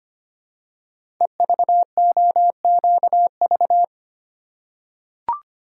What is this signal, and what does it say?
Morse code, heard as a keyed tone, about 25 wpm, 700 Hz. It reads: EVOQV
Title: Code Groups - 5 Letters - 25wpm